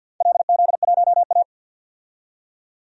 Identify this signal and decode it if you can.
Morse code, heard as a keyed tone, about 52 wpm, 700 Hz. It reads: LZ1A
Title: DXCC-100 Call Signs - 4 Characters - 50wpm